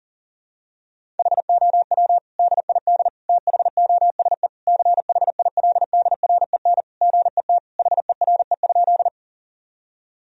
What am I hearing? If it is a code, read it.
Morse code, heard as a keyed tone, about 40 wpm, 700 Hz. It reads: HOW DID THOSE CHILDREN GET HERE?
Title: Sentences from Top 1000 Words - Review Part 1 - 40wpm